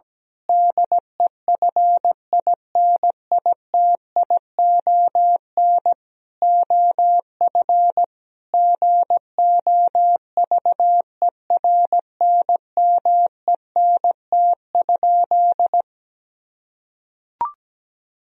Morse code, heard as a keyed tone, about 17 wpm, 700 Hz. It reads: DEFINITION OF GOVERNMENT?
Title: Sentences from Top 1000 Words - Review Part 4 - 17wpm